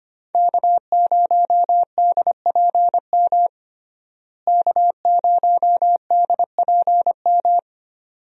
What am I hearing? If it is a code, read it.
Morse code, heard as a keyed tone, about 25 wpm, 700 Hz. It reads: K0DPM K0DPM
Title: USA Call Signs - 5 Characters - Repeated 2x - 25wpm